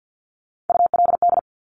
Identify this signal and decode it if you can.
Morse code, heard as a keyed tone, about 50 wpm, 700 Hz. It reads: ULD
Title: Top Three Letter Combinations in Words 50wpm